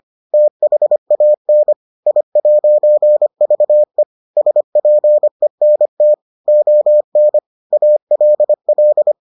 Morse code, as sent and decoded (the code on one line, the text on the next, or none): - .... .- -. / .. .----. ...- . / ... .--. -. - / --- -. / .- .-.. .-..
THAN I'VE SPENT ON ALL